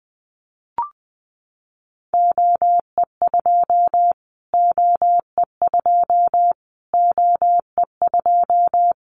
Morse code, heard as a keyed tone, about 20 wpm, 700 Hz. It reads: OE2 OE2 OE2
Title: Callsign Prefixes for Top 25 DXCC - Repeated 3x - 20wpm